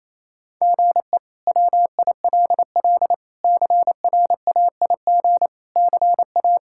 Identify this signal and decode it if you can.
Morse code, heard as a keyed tone, about 28 wpm, 700 Hz. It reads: GE WILL CRAIG CA
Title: K1USN SST QSO Practice - 28wpm